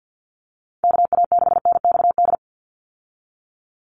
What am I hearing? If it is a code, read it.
Morse code, heard as a keyed tone, about 50 wpm, 700 Hz. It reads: KA6NXD
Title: USA Call Signs 50wpm